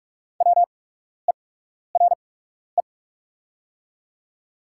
Morse code, heard as a keyed tone, about 44 wpm, spaced as for 12 wpm, 690 Hz. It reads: WERE
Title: 8x Character Spacing - Top 100 Words - 45wpm